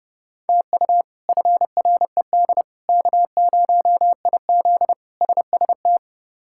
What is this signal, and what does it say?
Morse code, heard as a keyed tone, about 30 wpm, 700 Hz. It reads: TU FRED K0IZ SST
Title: K1USN SST QSO Practice - 30wpm